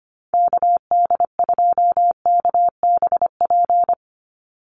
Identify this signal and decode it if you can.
Morse code, heard as a keyed tone, about 25 wpm, 700 Hz. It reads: KD2KBP
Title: USA Call Signs 25wpm